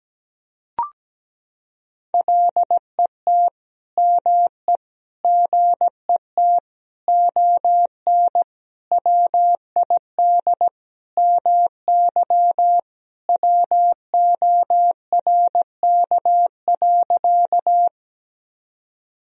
Morse code, heard as a keyed tone, about 17 wpm, 700 Hz. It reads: LET ME GET ON WID MY WORK.
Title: Sentences from Top 300 Words and Abbreviations 17wpm